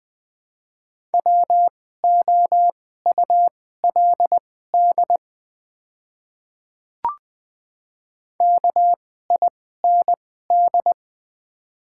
Morse code, heard as a keyed tone, about 20 wpm, spaced as for 14 wpm, 700 Hz. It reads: WOULD KIND
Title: 2x Character Spacing - Top 300 Words - 20wpm